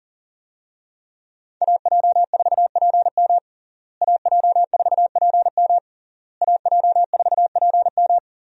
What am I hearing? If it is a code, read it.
Morse code, heard as a keyed tone, about 40 wpm, 700 Hz. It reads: AJ4PM AJ4PM AJ4PM
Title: USA Call Signs - 5 Characters - Repeated 3x - 40wpm